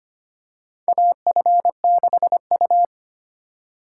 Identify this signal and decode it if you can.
Morse code, heard as a keyed tone, about 25 wpm, 700 Hz. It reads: AF6U
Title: USA Call Signs - 4 Characters - 2x Character Spacing - 25wpm